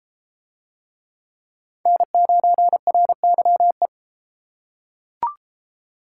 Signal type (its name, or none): Morse code